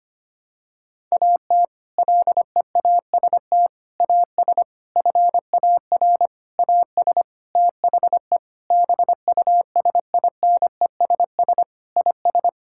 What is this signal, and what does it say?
Morse code, heard as a keyed tone, about 25 wpm, 700 Hz. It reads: AT LEAST AS FAR AS THE BUSINESS IS
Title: Sentences from Top 1000 Words - Review Part 6 - 25wpm